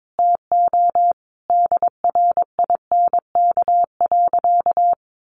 Morse code, heard as a keyed tone, about 22 wpm, 700 Hz. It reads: TO DRINK.
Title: Sentences from Top 1500 Words 22wpm